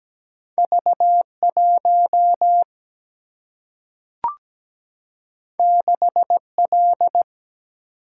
Morse code, heard as a keyed tone, about 17 wpm, 700 Hz. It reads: V1 6L
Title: Single-Letter Single-Number 17wpm